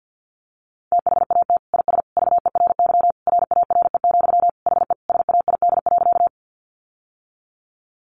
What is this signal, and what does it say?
Morse code, heard as a keyed tone, about 50 wpm, 700 Hz. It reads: THAT IS VERY RARE, HE SAID.
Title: Sentences from Top 1100 Words 50wpm